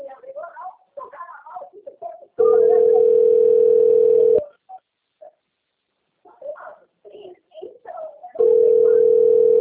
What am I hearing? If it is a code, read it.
Ringback tone (North America), 440 + 480 Hz, with speech.
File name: out- - - - - .68727